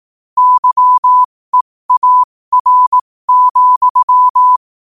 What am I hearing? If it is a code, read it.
Morse code, heard as a keyed tone, about 18 wpm, spaced as for 15 wpm, 1000 Hz. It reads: YEAR,
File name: qotdfiles / 15wpm